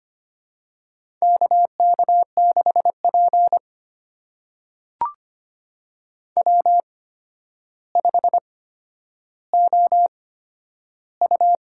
Morse code, heard as a keyed tone, about 25 wpm, spaced as for 10 wpm, 700 Hz. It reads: K K6P W5OU
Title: USA Call Signs - 4 Characters - 8x Character Spacing - 25wpm